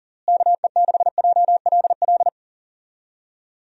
Morse code, heard as a keyed tone, about 40 wpm, 700 Hz. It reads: KE6JLL
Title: USA Call Signs - 6 Characters - Repeated 3x - 40wpm